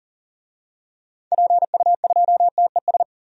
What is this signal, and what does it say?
Morse code, heard as a keyed tone, about 40 wpm, 700 Hz. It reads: PU2TES
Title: DXCC-100 Call Signs - 6 Characters - 40wpm